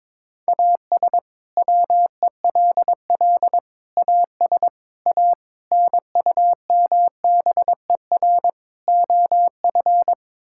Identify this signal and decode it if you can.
Morse code, heard as a keyed tone, about 22 wpm, 700 Hz. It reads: AS WELL AS A NUMBER OF
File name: practice / 22wpm / Sets of 6 Words - 22wpm